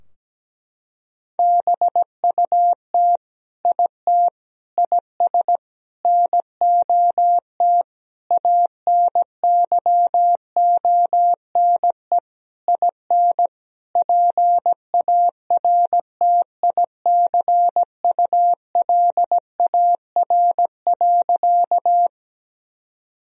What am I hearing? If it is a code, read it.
Morse code, heard as a keyed tone, about 17 wpm, 700 Hz. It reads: BUT IT IS NOT ANYONE IN PARTICULAR.
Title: Sentences from Top 1000 Words - Review Part 4 - 17wpm